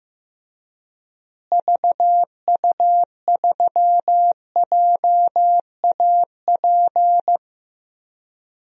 Morse code, heard as a keyed tone, about 15 wpm, 700 Hz. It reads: VU3JAP
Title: DXCC-100 Call Signs - 6 Characters - Repeated 3x - 15wpm